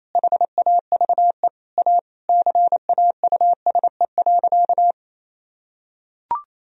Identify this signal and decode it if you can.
Morse code, heard as a keyed tone, about 27 wpm, 700 Hz. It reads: HAVE A CAUSE.
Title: Sentences from Top 1000 Words - Review Part 1 - 28wpm